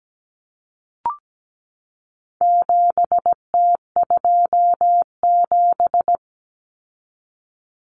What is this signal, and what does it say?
Morse code, heard as a keyed tone, about 17 wpm, 700 Hz. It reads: 7T27
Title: Common Cut Numbers - 4-Digit 17wpm